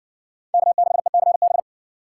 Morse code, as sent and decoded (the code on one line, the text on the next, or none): -.- -.... -.-. -...
K6ECB